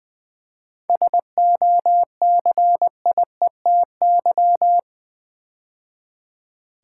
Morse code, heard as a keyed tone, about 20 wpm, 700 Hz. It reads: SOCIETY
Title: 7 Letter Words - First 3 Letters Given - 20wpm